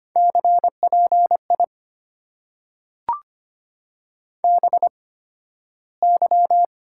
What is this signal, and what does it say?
Morse code, heard as a keyed tone, about 25 wpm, spaced as for 13 wpm, 700 Hz. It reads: CPI BY
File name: Top 90 QSO Elements - 8x Character Spacing - 25wpm